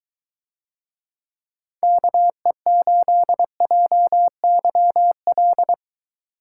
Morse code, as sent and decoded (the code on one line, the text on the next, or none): -.- . ---.. .--- -.-- .-..
KE8JYL